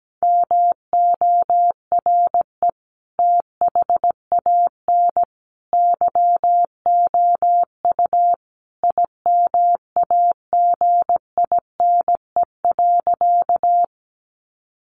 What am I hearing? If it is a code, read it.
Morse code, heard as a keyed tone, about 17 wpm, 700 Hz. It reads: MORE THAN YOU IMAGINE.
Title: Sentences from Top 1400 Words 17wpm